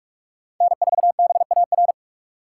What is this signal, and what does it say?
Morse code, heard as a keyed tone, about 44 wpm, 700 Hz. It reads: N4BAR
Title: USA Call Signs - 5 Characters - 6x Character Spacing - 45wpm